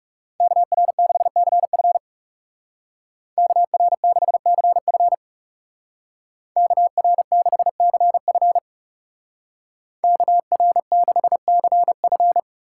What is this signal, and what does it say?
Morse code, heard as a keyed tone, about 37 wpm, 700 Hz. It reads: KR6CF KR6CF KR6CF KR6CF
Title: Speed Racing - USA 2x2 Call Signs 30wpm